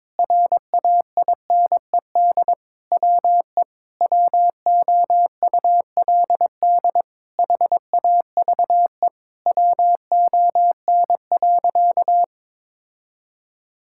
Morse code, heard as a keyed tone, about 22 wpm, 700 Hz. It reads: RAINED WE WOULD HAVE WON.
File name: Sentences with 7-Word Phrases - 22wpm